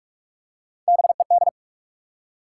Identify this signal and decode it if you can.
Morse code, heard as a keyed tone, about 47 wpm, 700 Hz. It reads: BED
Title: Three Letter Words - 45wpm